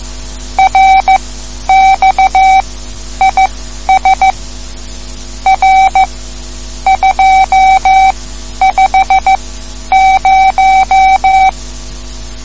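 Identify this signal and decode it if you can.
Morse code, heard as a keyed tone, about 15 wpm, spaced as for 10 wpm, 770 Hz. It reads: RX IS R250
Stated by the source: Morse code lesson, third lesson in a series for learning the Morse alphabet.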